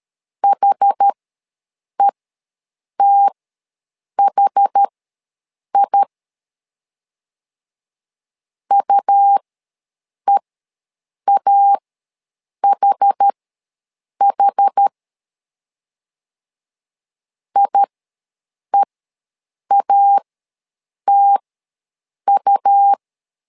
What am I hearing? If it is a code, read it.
Morse code, heard as a keyed tone, about 13 wpm, spaced as for 7 wpm, 800 Hz. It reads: HETHI UEAHH IEATU
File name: LEZIONI / 2 UHDN / 2 UH ws 12wpm 1730